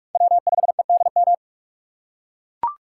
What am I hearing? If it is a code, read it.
Morse code, heard as a keyed tone, about 43 wpm, 700 Hz. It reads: W5EDM
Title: USA Call Signs - 5 Characters - 2x Character Spacing - 45wpm